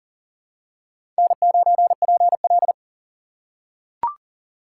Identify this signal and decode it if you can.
Morse code, heard as a keyed tone, about 40 wpm, 700 Hz. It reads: N9PL
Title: USA Call Signs - 4 Characters - 3x Character Spacing - 40wpm